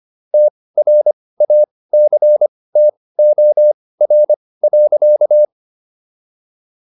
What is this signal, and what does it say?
Morse code, heard as a keyed tone, about 25 wpm, spaced as for 18 wpm, 600 Hz. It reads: TRACTOR.